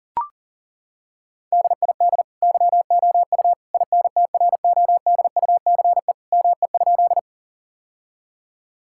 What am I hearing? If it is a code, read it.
Morse code, heard as a keyed tone, about 37 wpm, 700 Hz. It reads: DID YOU INTRODUCE ME?
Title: Sentences from Top 1600 Words 40wpm